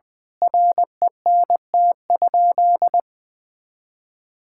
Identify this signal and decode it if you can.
Morse code, heard as a keyed tone, about 20 wpm, 700 Hz. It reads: RENT?
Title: Sentences from Top 1000 Words - Review Part 5 - 20wpm